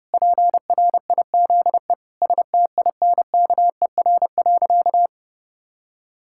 Morse code, heard as a keyed tone, about 30 wpm, 700 Hz. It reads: PRIZE STINKER.
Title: Pangram Challenge v2 - 30wpm